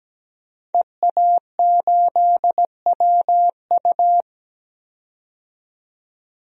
Morse code, heard as a keyed tone, about 17 wpm, 700 Hz. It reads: EA8WU
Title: DXCC-100 Call Signs - 5 Characters - 17wpm